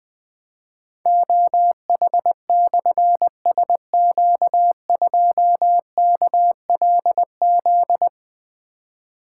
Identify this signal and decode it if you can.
Morse code, heard as a keyed tone, about 20 wpm, 700 Hz. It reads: OH/SQ2KLZ
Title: Difficult Call Signs 20wpm